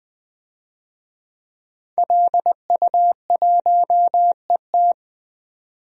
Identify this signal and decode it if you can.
Morse code, heard as a keyed tone, about 20 wpm, 700 Hz. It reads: LU1ET
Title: DXCC-100 Call Signs - 5 Characters - 20wpm